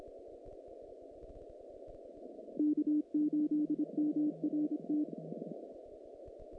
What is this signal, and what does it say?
Morse code, heard as a keyed tone, about 25 wpm, 300 Hz. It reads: K8MRT
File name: contests / naqp-cw-k5tr-op-k5pi- -2200